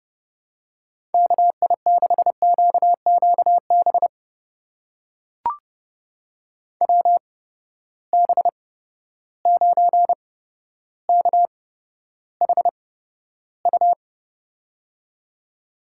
Morse code, heard as a keyed tone, about 30 wpm, spaced as for 13 wpm, 700 Hz. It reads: KI6QQB WB9KHU
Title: USA Call Signs - 6 Characters - 8x Character Spacing - 30wpm